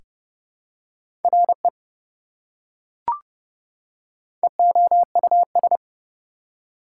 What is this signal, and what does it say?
Morse code, heard as a keyed tone, about 30 wpm, 700 Hz. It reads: RE EOUS
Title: Prefixes and Suffixes 30wpm